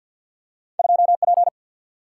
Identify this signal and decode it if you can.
Morse code, heard as a keyed tone, about 53 wpm, 700 Hz. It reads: 2 P
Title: Single-Letter Single-Number 50wpm